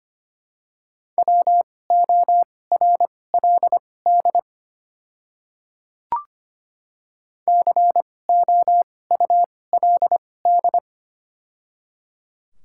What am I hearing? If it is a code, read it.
Morse code, heard as a keyed tone, about 25 wpm, spaced as for 18 wpm, 700 Hz. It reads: WORLD COULD
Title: 2x Character Spacing - Top 300 Words - 25wpm